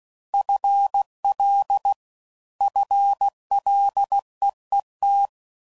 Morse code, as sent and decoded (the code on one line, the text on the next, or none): ..-. .-.. / ..-. .-.. . -
FL FLEET